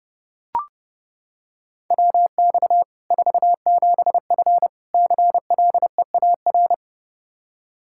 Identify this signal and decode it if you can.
Morse code, heard as a keyed tone, about 30 wpm, 700 Hz. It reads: WX 47F CLEAR
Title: Practice QSO - Exchange 2 - Info - 30wpm